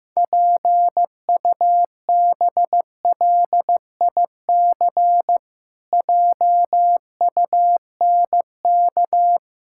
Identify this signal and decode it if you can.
Morse code, heard as a keyed tone, about 15 wpm, 700 Hz. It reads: PUBLIC JUNK
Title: Pangram Challenge v2 - 15wpm